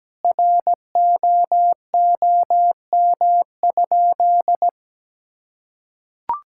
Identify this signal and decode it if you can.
Morse code, heard as a keyed tone, about 17 wpm, 700 Hz. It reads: ROOM?
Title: Sentences from Top 1000 Words - Review Part 6 - 17wpm